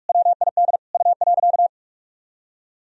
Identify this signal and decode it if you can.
Morse code, heard as a keyed tone, about 46 wpm, 700 Hz. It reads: WID U.
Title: Sentences from Top 200 Words and Abbreviations 45wpm